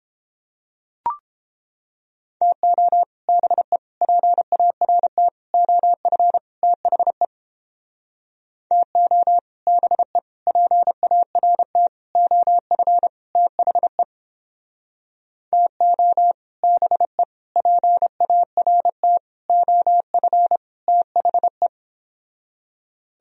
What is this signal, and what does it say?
Morse code, heard as a keyed tone, about 29 wpm, 700 Hz. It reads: TO BE PART OF THE TO BE PART OF THE TO BE PART OF THE